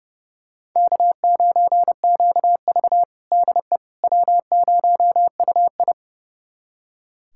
Morse code, heard as a keyed tone, about 30 wpm, 700 Hz. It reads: K9QV DE W0UI
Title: USA Callsign DE Callsign - 4 Characters - 30wpm